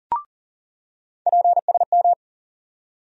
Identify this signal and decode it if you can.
Morse code, heard as a keyed tone, about 40 wpm, 700 Hz. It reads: PSM
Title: USA Call Sign Postfixes - 3 Letters 40wpm